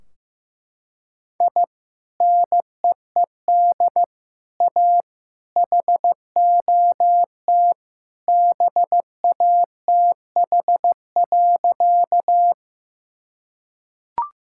Morse code, heard as a keyed tone, about 16 wpm, 700 Hz. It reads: I NEED A HOT BATH.
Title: Sentences from Top 1600 Words 15wpm